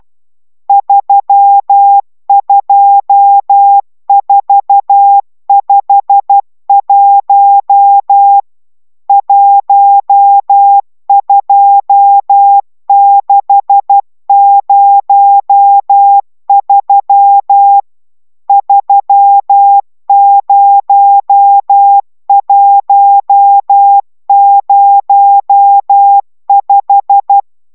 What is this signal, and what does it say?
Morse code, heard as a keyed tone, about 12 wpm, 800 Hz. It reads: 32451 12603 30105
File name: taviro tanulas / cw 69 22 10